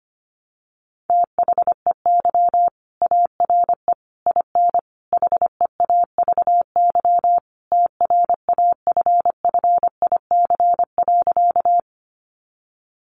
Morse code, heard as a keyed tone, about 25 wpm, 700 Hz. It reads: THEY ARE IN HEAVY TRAFFIC.
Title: Sentences from Top 1300 Words 25wpm